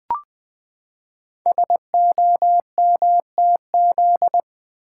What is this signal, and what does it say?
Morse code, heard as a keyed tone, about 20 wpm, 700 Hz. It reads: SOMTZ
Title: Code Groups - 5 Letter:Numbers - 20wpm